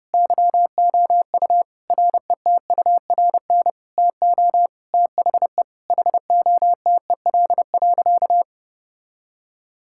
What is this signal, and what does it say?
Morse code, heard as a keyed tone, about 31 wpm, 700 Hz. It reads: YOU RETURN TO THE HOTEL.
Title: Sentences from Top 1600 Words 30wpm